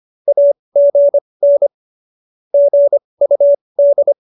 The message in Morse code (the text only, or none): AGN GUD